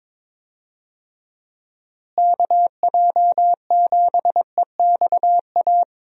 Morse code, heard as a keyed tone, about 22 wpm, 700 Hz. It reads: KJ7EXA
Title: USA Call Signs - 6 Characters - Repeated 2x - 22wpm